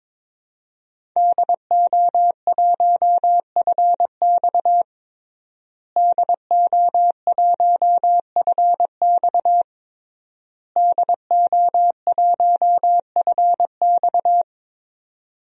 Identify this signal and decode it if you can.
Morse code, heard as a keyed tone, about 22 wpm, 700 Hz. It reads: DO1FX DO1FX DO1FX
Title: DXCC-100 Call Signs - 5 Characters - Repeated 3x - 22wpm